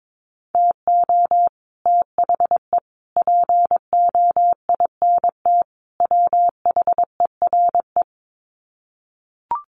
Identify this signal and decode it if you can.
Morse code, heard as a keyed tone, about 22 wpm, 700 Hz. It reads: TO THE POINT WHERE
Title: Sets of 4 Words 22wpm